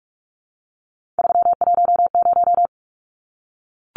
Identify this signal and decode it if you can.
Morse code, heard as a keyed tone, about 44 wpm, 700 Hz. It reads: 310
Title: Numbers - 3-Digit 45wpm